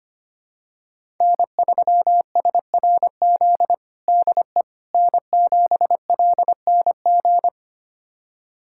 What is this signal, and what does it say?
Morse code, heard as a keyed tone, about 25 wpm, 700 Hz. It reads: N3SRZ DE N7LNG
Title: USA Callsign DE Callsign - 5 Characters - 25wpm